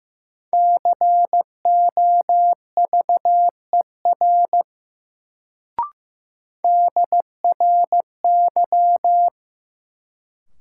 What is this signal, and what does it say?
Morse code, heard as a keyed tone, about 15 wpm, 700 Hz. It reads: COVER DRY
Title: Sentences from Top 500 Words - New Word Drill - 15wpm